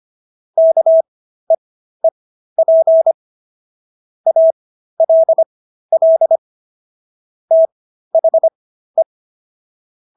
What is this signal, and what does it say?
Morse code, heard as a keyed tone, about 25 wpm, spaced as for 13 wpm, 650 Hz. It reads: KEEP ALL THE